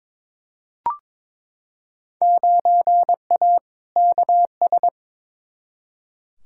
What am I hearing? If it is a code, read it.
Morse code, heard as a keyed tone, about 22 wpm, 700 Hz. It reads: 9A KS
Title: Field Day Exchanges Encore - 22wpm